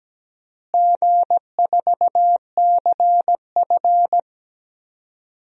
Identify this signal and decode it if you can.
Morse code, heard as a keyed tone, about 17 wpm, 700 Hz. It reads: G4CF
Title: DXCC-100 Call Signs - 4 Characters - Repeated 3x - 17wpm